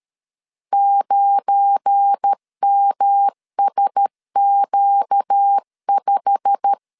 Morse code, heard as a keyed tone, about 13 wpm, 800 Hz. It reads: 9MSQ5
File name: LEZIONI / 9 2 12wpm 2301